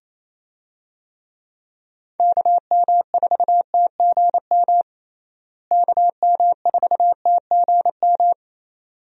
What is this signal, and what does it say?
Morse code, heard as a keyed tone, about 28 wpm, 700 Hz. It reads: KM4TGM KM4TGM
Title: USA Call Signs - 6 Characters - Repeated 2x - 28wpm